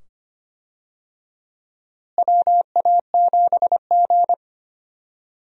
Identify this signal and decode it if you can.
Morse code, heard as a keyed tone, about 25 wpm, 700 Hz. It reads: WA7G
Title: USA Call Signs - 4 Characters - Repeated 2x - 25wpm